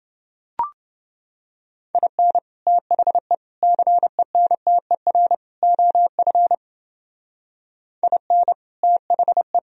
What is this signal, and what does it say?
Morse code, heard as a keyed tone, about 30 wpm, 700 Hz. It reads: IN THE CENTER OF IN THE